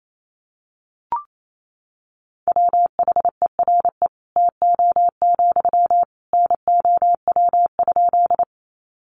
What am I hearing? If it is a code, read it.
Morse code, heard as a keyed tone, about 28 wpm, 700 Hz. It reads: WHERE TO, NOW?
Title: Sentences from Top 200 Words 28wpm